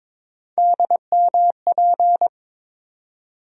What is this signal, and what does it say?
Morse code, heard as a keyed tone, about 22 wpm, 700 Hz. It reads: DMP
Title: USA Call Sign Postfixes - 3 Letters 22wpm